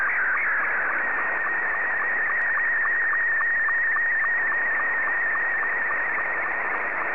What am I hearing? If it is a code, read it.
SSTV (slow-scan television), Scottie 2.